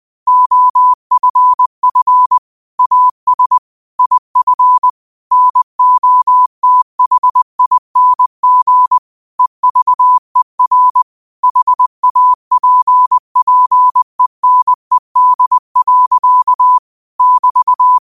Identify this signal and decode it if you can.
Morse code, heard as a keyed tone, about 20 wpm, 1000 Hz. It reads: OFF AS IF NOTHING EVER HAPPENED. <BT>